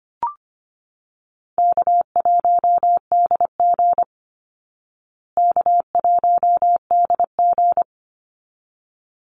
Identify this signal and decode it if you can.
Morse code, heard as a keyed tone, about 25 wpm, 700 Hz. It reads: K1DG K1DG
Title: Easy Call Signs Repeated 2x - 25wpm